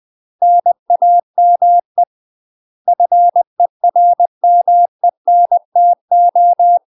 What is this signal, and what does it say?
Morse code, heard as a keyed tone, about 20 wpm, 700 Hz. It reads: NAME FERMENTO
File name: UK MASTER QSOs TRAINING / 20WPM / UK MASTER QSOs 0025